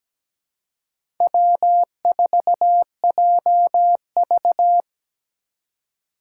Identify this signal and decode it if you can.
Morse code, heard as a keyed tone, about 17 wpm, 700 Hz. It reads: W4JV
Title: USA Call Signs - 4 Characters - 2x Character Spacing - 17wpm